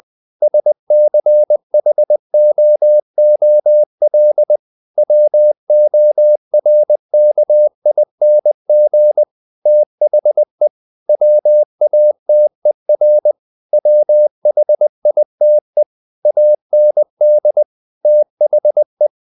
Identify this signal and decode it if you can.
Morse code, heard as a keyed tone, about 20 wpm, 600 Hz. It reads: SCHOOL WORKING THE WATER WHITE AND THE
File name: the old man and the sea / the old man and the sea part 1 20wpm